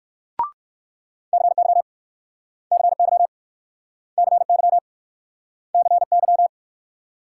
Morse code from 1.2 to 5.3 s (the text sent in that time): CY CY CY